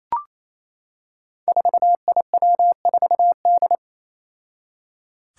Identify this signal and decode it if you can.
Morse code, heard as a keyed tone, about 27 wpm, 700 Hz. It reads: 4IW4D
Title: Five Challenging Letter-Number Combinations 28wpm